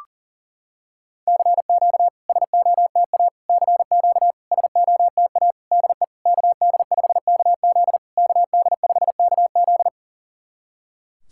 Morse code, heard as a keyed tone, about 40 wpm, 690 Hz. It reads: CQ SOTA CQ SOTA DE KD5KZ KD5KZ
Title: SOTA QSO Practice 40wpm